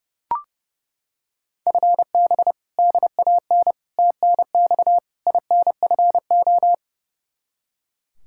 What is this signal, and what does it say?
Morse code, heard as a keyed tone, about 29 wpm, 700 Hz. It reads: FB DAN TNX INFO
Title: Practice QSO - Exchange 3 - More Info - 30wpm